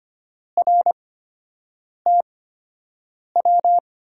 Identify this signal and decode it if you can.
Morse code, heard as a keyed tone, about 25 wpm, spaced as for 7 wpm, 700 Hz. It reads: RTW
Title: USA Call Signs - 6 Characters - 8x Character Spacing - 25wpm